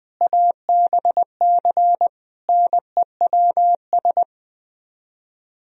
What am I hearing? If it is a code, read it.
Morse code, heard as a keyed tone, about 20 wpm, 700 Hz. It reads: ABC NEWS
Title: Speed-Racing - Sets of 5 Words 20wpm